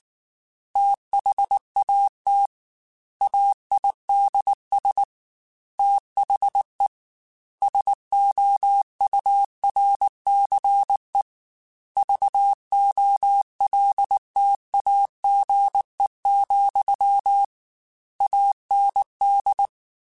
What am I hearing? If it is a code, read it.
Morse code, heard as a keyed tone, about 19 wpm, 800 Hz. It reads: THAT AIDS THE SOURCE VOLTAGE, AND